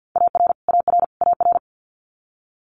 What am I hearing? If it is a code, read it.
Morse code, heard as a keyed tone, about 50 wpm, 690 Hz. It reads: AR AR AR
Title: Multi-letter Phonemes Repeated 3x 50wpm